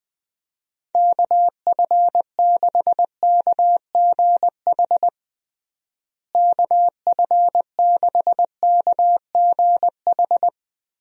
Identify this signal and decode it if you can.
Morse code, heard as a keyed tone, about 20 wpm, 700 Hz. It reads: KF6KGH KF6KGH
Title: USA Call Signs - 6 Characters - Repeated 3x - 20wpm